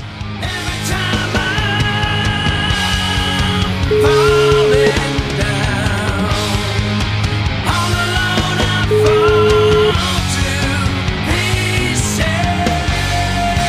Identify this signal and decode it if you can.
Ringback tone (Europe, 425 Hz), with noise.